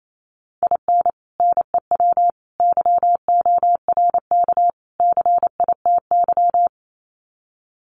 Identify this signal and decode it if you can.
Morse code, heard as a keyed tone, about 28 wpm, 700 Hz. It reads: IN NEW YORK CITY
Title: Sets of 4 Words 28wpm